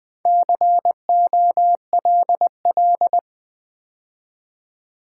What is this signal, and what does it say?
Morse code, heard as a keyed tone, about 20 wpm, 700 Hz. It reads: COLL
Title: Build A Word - 2 Letters At A Time - 8-Letter Words - 20wpm